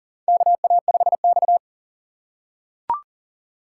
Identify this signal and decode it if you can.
Morse code, heard as a keyed tone, about 40 wpm, 700 Hz. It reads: KA5X
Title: USA Call Signs - 4 Characters - 3x Character Spacing - 40wpm